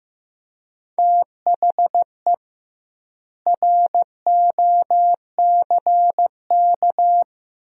Morse code, heard as a keyed tone, about 16 wpm, 700 Hz. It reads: THE ROCK
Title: 2x Word-Spacing - Sentences from Top 400 Words - 15wpm